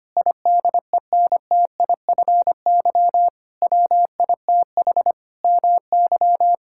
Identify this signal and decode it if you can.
Morse code, heard as a keyed tone, about 25 wpm, 700 Hz. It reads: IDENTIFY WITH MY
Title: Sentences from Top 1000 Words - Review Part 6 - 25wpm